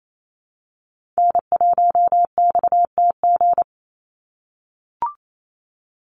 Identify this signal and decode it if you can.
Morse code, heard as a keyed tone, about 28 wpm, 700 Hz. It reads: N1XTG
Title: USA Call Signs 28wpm